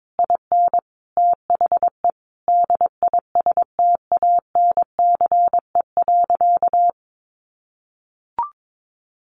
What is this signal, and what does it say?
Morse code, heard as a keyed tone, about 22 wpm, 700 Hz. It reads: IN THE DISTANCE.
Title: Sentences from Top 1500 Words 22wpm